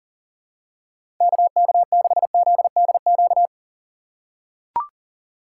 Morse code, heard as a keyed tone, about 40 wpm, 690 Hz. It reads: KK6ZDQ
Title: USA Call Signs - 6 Characters - 2x Character Spacing - 40wpm